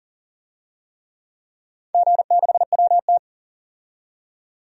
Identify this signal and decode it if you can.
Morse code, heard as a keyed tone, about 40 wpm, 700 Hz. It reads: G6WT
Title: DXCC-100 Call Signs - 4 Characters - 40wpm